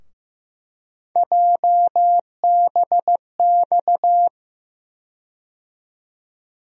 Morse code, heard as a keyed tone, about 15 wpm, 700 Hz. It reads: JBX
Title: USA Call Sign Postfixes - 3 Letters 15wpm